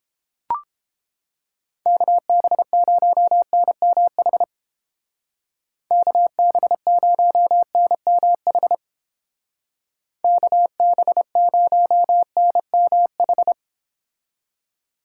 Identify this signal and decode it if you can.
Morse code, heard as a keyed tone, about 30 wpm, 700 Hz. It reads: KB0NMH KB0NMH KB0NMH